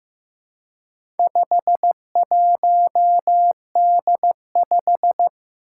Morse code, heard as a keyed tone, about 15 wpm, 700 Hz. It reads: 51D5
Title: Four Challenging Letter-Number Combinations 15wpm